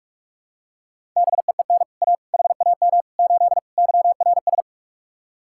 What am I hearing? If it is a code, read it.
Morse code, heard as a keyed tone, about 46 wpm, 700 Hz. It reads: BEEN A HAM 8 YRS